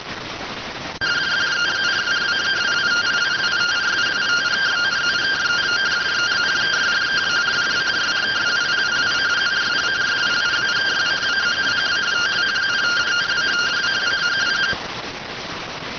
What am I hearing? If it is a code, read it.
Two-tone teleprinter signal, about 170 Hz shift.